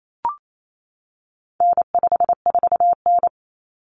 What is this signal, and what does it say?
Morse code, heard as a keyed tone, about 28 wpm, 700 Hz. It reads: N54N
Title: Common Cut Numbers - 4-Digit 28wpm